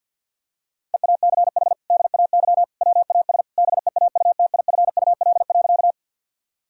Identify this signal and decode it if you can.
Morse code, heard as a keyed tone, about 51 wpm, 700 Hz. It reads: EACH DAY WAS BEAUTIFUL.
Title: Sentences from Top 1000 Words - Review Part 1 - 50wpm